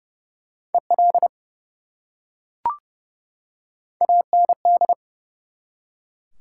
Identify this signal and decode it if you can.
Morse code, heard as a keyed tone, about 30 wpm, 700 Hz. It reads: EL AND
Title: Top 180 QSO Elements 30wpm